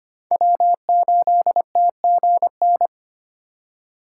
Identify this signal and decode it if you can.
Morse code, heard as a keyed tone, about 32 wpm, 700 Hz. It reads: W8TGN